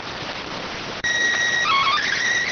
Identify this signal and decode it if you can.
SSTV (slow-scan television), PD 290.